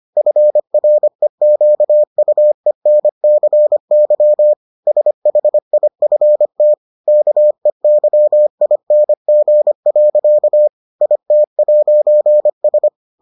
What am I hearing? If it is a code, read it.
Morse code, heard as a keyed tone, about 25 wpm, 600 Hz. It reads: FREQUENCY SHIFT KEYING. IT'S